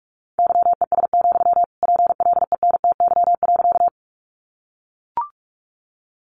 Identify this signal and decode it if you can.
Morse code, heard as a keyed tone, about 44 wpm, 700 Hz. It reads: YES, PLENTY.
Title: Sentences from Top 1300 Words 45wpm